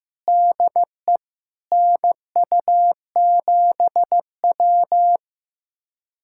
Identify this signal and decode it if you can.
Morse code, heard as a keyed tone, about 15 wpm, 700 Hz. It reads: DE NU7W
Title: USA Callsign DE Callsign - 4 Characters - 15wpm